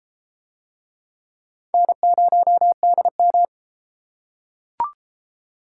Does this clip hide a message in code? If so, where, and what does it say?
Yes: Morse code from 1.7 to 3.5 s, heard as a keyed tone, about 32 wpm, 700 Hz. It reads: N0DM